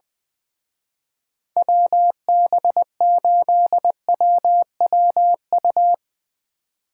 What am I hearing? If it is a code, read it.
Morse code, heard as a keyed tone, about 20 wpm, 700 Hz. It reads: WB8WWU
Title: Speed Racing - USA 2x3 Call Signs 17wpm